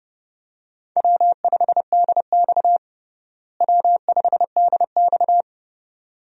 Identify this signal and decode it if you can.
Morse code, heard as a keyed tone, about 32 wpm, 700 Hz. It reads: W5DX W5DX
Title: USA Call Signs - 4 Characters - Repeated 2x - 30wpm